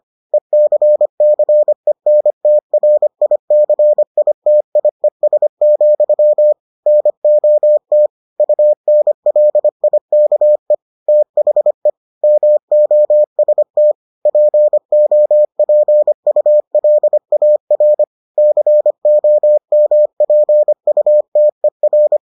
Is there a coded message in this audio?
Yes: Morse code from 0.3 to 22.2 s, heard as a keyed tone, about 25 wpm, 600 Hz. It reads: ECCENTRICITIES, NOT UNLIKE THE MOST POPULAR COMPUTER